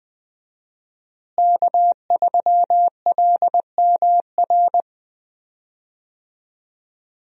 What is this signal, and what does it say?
Morse code, heard as a keyed tone, about 20 wpm, 700 Hz. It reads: K3LMR
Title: USA Call Signs - 5 Characters - Repeated 2x - 20wpm